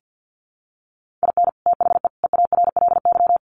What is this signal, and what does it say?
Morse code, heard as a keyed tone, about 48 wpm, 700 Hz. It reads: IN THE EARLY
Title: Sets of 3 Words 50wpm